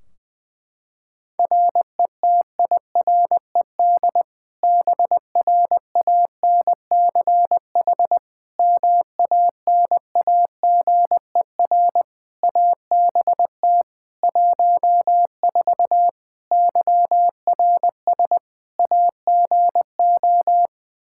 Morse code, heard as a keyed tone, about 20 wpm, 700 Hz. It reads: RETIRED BRANCH MANAGER ABT 14 YRS AGO
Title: Practice QSO - Exchange 3 - More Info - 20wpm